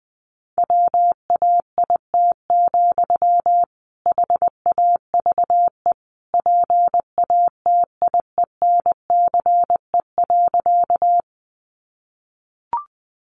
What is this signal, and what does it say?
Morse code, heard as a keyed tone, about 20 wpm, 700 Hz. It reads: WAIT, HAVE PATIENCE.
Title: Sentences from Top 1300 Words 20wpm